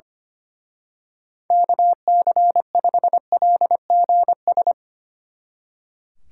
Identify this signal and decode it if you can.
Morse code, heard as a keyed tone, about 25 wpm, 700 Hz. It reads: KC5LGS
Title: Speed Racing - USA 2x3 Call Signs 25wpm